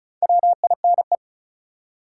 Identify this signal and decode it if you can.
Morse code, heard as a keyed tone, about 36 wpm, 700 Hz. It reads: WINE
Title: Four Letter Words - 35wpm